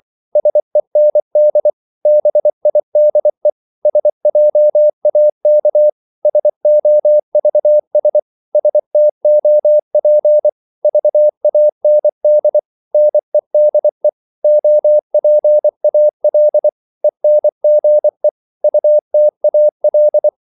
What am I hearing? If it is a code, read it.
Morse code, heard as a keyed tone, about 24 wpm, 600 Hz. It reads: SEND BIDE SJAK SOVS STOP VAND NEDE OPAL ENGE UTAL
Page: Korte danske ord 24wpm